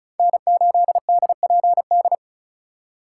Morse code, heard as a keyed tone, about 36 wpm, 700 Hz. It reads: N8DPD
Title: K1USN SST QSO Practice - 35wpm